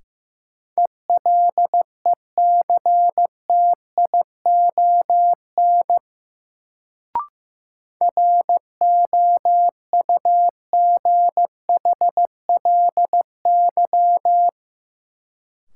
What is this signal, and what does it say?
Morse code, heard as a keyed tone, about 15 wpm, 700 Hz. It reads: ELECTION ROUGHLY
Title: Sentences from Top 1500 Words - New Word Drill - 15wpm